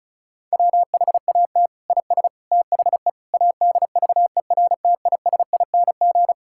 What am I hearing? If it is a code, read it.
Morse code, heard as a keyed tone, about 36 wpm, 700 Hz. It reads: WHAT IS THE ADVERTISING
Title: Sentences from Top 1000 Words - Review Part 6 - 35wpm